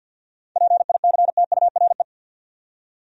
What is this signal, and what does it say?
Morse code, heard as a keyed tone, about 63 wpm, 700 Hz. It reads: PICTURE